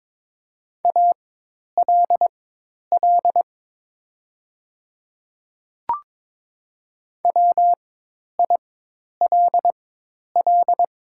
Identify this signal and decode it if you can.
Morse code, heard as a keyed tone, about 22 wpm, spaced as for 10 wpm, 700 Hz. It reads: ALL WILL
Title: 4x Character Spacing - Top 100 Words - 22wpm